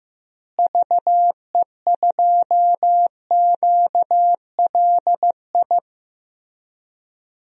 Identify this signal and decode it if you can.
Morse code, heard as a keyed tone, about 15 wpm, 700 Hz. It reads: VE2QLI
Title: DXCC-100 Call Signs - 6 Characters - Repeated 3x - 15wpm